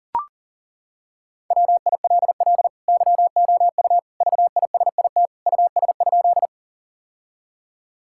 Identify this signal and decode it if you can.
Morse code, heard as a keyed tone, about 37 wpm, 700 Hz. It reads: WILL YOU VISIT US?
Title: Sentences from Top 1000 Words - Review Part 6 - 40wpm